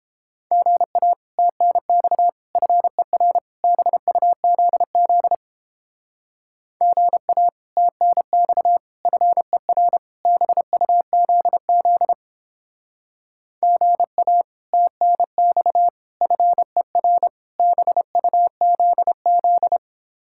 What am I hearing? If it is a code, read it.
Morse code, heard as a keyed tone, about 29 wpm, 700 Hz. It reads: GA TNX FER BUZZ GA TNX FER BUZZ GA TNX FER BUZZ